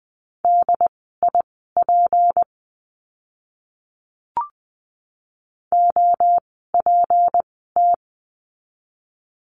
Morse code, heard as a keyed tone, about 20 wpm, spaced as for 15 wpm, 700 Hz. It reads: DIP OPT
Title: 153 Lesson 25f - Letter p - Words 20wpm